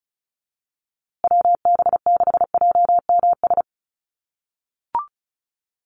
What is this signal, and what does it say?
Morse code, heard as a keyed tone, about 34 wpm, 700 Hz. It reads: WB6JMS
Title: USA Call Signs Repeated 2x 35wpm